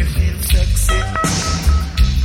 There is no code.